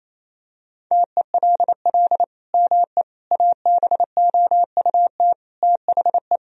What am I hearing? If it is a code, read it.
Morse code, heard as a keyed tone, about 28 wpm, 700 Hz. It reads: TELL ME ABOUT THE
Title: Sentences from Top 1000 Words - Review Part 3 - 28wpm